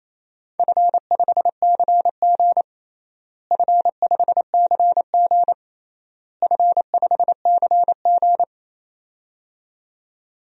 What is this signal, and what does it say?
Morse code, heard as a keyed tone, about 28 wpm, 700 Hz. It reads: F5CG F5CG F5CG
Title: DXCC-100 Call Signs - 4 Characters - Repeated 3x - 28wpm